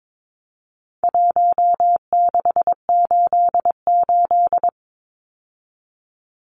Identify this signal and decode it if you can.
Morse code, heard as a keyed tone, about 22 wpm, 700 Hz. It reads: 1688
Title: Numbers - Contest Style - 4-Digit 22wpm